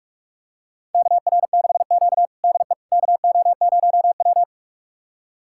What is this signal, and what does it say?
Morse code, heard as a keyed tone, about 46 wpm, 700 Hz. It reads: KR6Q DE KO0W
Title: USA Callsign DE Callsign - 4 Characters - 45wpm